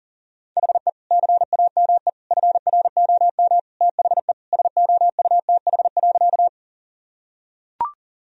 Morse code, heard as a keyed tone, about 40 wpm, 700 Hz. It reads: HE CAME FROM THE SOUTH.
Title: Sentences from Top 1000 Words - Review Part 2 - 40wpm